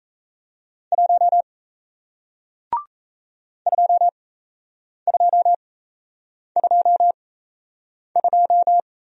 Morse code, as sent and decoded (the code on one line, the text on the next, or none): .---- / ..--- ..--- ..--- ..---
1 2222